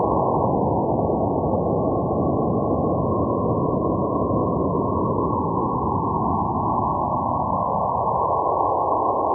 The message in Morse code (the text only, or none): none